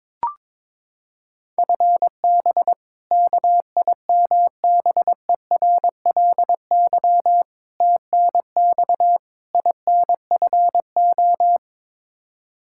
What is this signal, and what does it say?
Morse code, heard as a keyed tone, about 22 wpm, 700 Hz. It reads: FB KIMBERLY TNX INFO
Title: Practice QSO - Exchange 3 - More Info - 22wpm